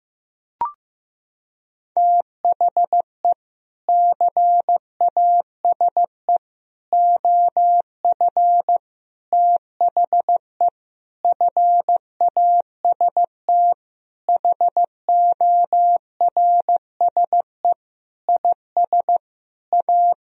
Morse code, heard as a keyed tone, about 15 wpm, 700 Hz. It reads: THE CASE OF THE FAST HORSE IS A
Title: Sentences from Top 1000 Words - Review Part 2 - 15wpm